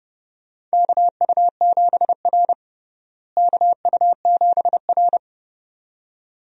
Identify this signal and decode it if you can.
Morse code, heard as a keyed tone, about 30 wpm, 700 Hz. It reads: KU7R KU7R
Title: USA Call Signs - 4 Characters - Repeated 2x - 30wpm